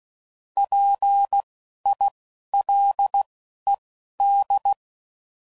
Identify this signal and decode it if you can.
Morse code, heard as a keyed tone, about 16 wpm, spaced as for 12 wpm, 800 Hz. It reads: PILED
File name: Prefixes / By First Letter / Ps